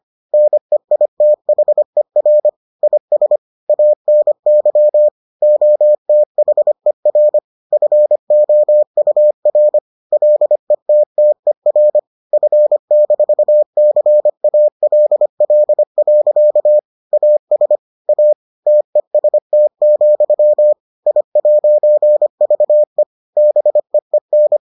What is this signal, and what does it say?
Morse code, heard as a keyed tone, about 25 wpm, 600 Hz. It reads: NEITHER IS ANY OTHER FOUR LETTER F-CALL. AS A TEST, I'VE BEEN